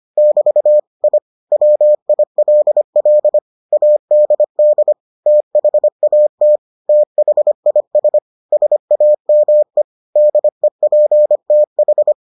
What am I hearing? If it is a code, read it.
Morse code, heard as a keyed tone, about 25 wpm, 600 Hz. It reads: <BT> I WILL ADD THAT THIS SAME DEPTH